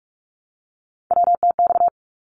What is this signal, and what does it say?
Morse code, heard as a keyed tone, about 44 wpm, 700 Hz. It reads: WTX
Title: ARRL Sections 45wpm